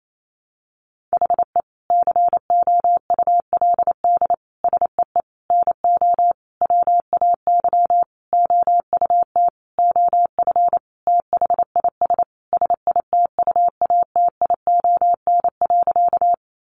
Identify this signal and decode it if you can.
Morse code, heard as a keyed tone, about 28 wpm, 700 Hz. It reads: HE COULD SEE NO WAY OUT OF THIS SITUATION.
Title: Sentences from Top 1200 Words 28wpm